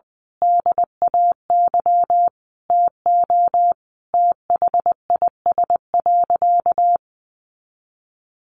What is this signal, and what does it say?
Morse code, heard as a keyed tone, about 20 wpm, 700 Hz. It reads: DAY TO THIS.
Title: Sentences from Top 1400 Words 20wpm